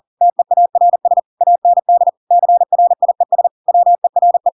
Morse code, heard as a keyed tone, about 40 wpm, 700 Hz. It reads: TEARS AND CRIES WERE
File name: Pinocchio / Pinocchio ch7 40 40 ik1hge